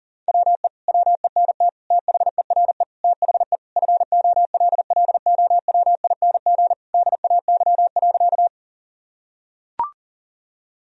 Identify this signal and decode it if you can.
Morse code, heard as a keyed tone, about 40 wpm, 700 Hz. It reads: WE WENT THERE THE FOLLOWING DAY.
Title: Sentences from Top 1000 Words - Review Part 4 - 40wpm